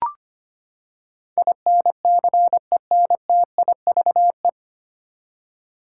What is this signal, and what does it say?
Morse code, heard as a keyed tone, about 25 wpm, 700 Hz. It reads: INCENTIVE
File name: Nine Letter Words - 25wpm